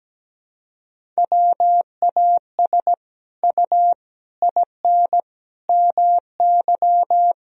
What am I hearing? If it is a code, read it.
Morse code, heard as a keyed tone, about 17 wpm, 700 Hz. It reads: WAS U IN MY
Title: Sentences from Top 300 Words and Abbreviations 17wpm